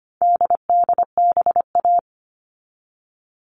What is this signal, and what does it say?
Morse code, heard as a keyed tone, about 25 wpm, 700 Hz. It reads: DDBA
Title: All Cut Numbers - 4-Digit 25wpm